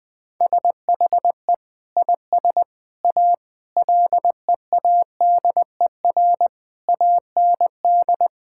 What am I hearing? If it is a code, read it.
Morse code, heard as a keyed tone, about 20 wpm, 700 Hz. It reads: SHE IS A LEADER AND
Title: Sentences with 5-Word Phrases - 20wpm